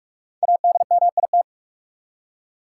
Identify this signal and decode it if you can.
Morse code, heard as a keyed tone, about 46 wpm, 700 Hz. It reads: ADMIT
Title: Five Letter Words - 45wpm